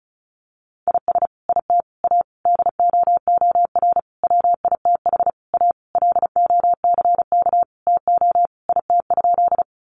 Morse code, heard as a keyed tone, about 34 wpm, 700 Hz. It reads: IS IT A DOOR WITH A LOCK TO IT?
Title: Sentences from Top 1300 Words 35wpm